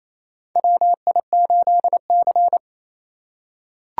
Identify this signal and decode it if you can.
Morse code, heard as a keyed tone, about 28 wpm, 700 Hz. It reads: WI8C
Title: USA Call Signs - 4 Characters - 2x Character Spacing - 28wpm